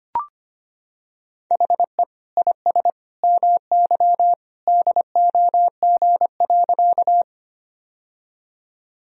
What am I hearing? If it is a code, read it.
Morse code, heard as a keyed tone, about 25 wpm, 700 Hz. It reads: HE IS MY DOG.
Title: Sentences from Top 1000 Words - Review Part 2 - 25wpm